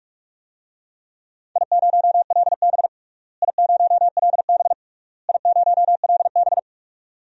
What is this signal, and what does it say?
Morse code, heard as a keyed tone, about 46 wpm, 700 Hz. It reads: I0LB I0LB I0LB
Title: DXCC-100 Call Signs - 4 Characters - Repeated 3x - 45wpm